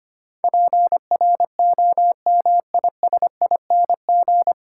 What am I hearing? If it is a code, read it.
Morse code, heard as a keyed tone, about 25 wpm, 700 Hz. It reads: PROMISING
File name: practice / 25wpm / 9 Letter Words - First 3 Letters Given - 25wpm